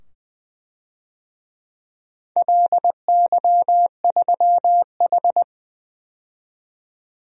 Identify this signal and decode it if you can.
Morse code, heard as a keyed tone, about 20 wpm, 700 Hz. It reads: LY3H
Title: DXCC-100 Call Signs - 4 Characters - 20wpm